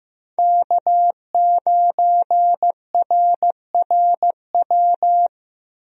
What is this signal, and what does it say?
Morse code, heard as a keyed tone, about 15 wpm, 700 Hz. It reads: K9RRW
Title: USA Call Signs - 5 Characters - 15wpm